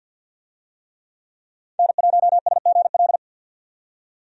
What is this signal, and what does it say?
Morse code, heard as a keyed tone, about 53 wpm, 700 Hz. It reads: N1SGL
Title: USA Call Signs - 5 Characters - 50wpm